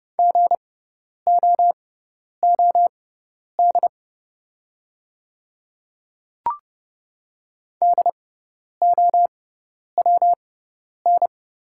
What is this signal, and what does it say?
Morse code, heard as a keyed tone, about 29 wpm, spaced as for 10 wpm, 690 Hz. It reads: GOOD DOWN
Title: 6x Character Spacing - Top 200 Words - 30wpm